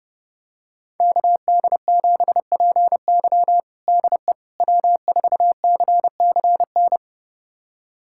Morse code, heard as a keyed tone, about 30 wpm, 700 Hz. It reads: KD7PY DE W4CCN
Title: USA Callsign DE Callsign - 5 Characters - 30wpm